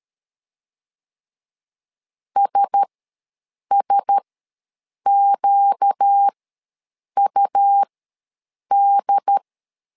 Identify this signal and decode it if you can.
Morse code, heard as a keyed tone, about 13 wpm, spaced as for 7 wpm, 800 Hz. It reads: SSQUD
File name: LEZIONI / 4 Q1S ws 12wpm 1827